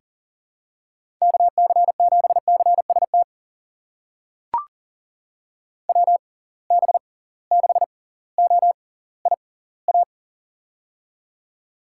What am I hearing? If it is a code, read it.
Morse code, heard as a keyed tone, about 40 wpm, spaced as for 20 wpm, 700 Hz. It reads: KC7CST WB6OIA
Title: USA Call Signs - 6 Characters - 6x Character Spacing - 40wpm